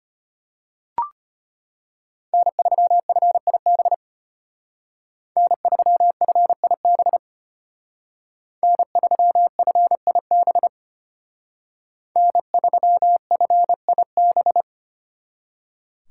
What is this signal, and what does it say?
Morse code, heard as a keyed tone, about 32 wpm, 700 Hz. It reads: N3FIB N3FIB N3FIB N3FIB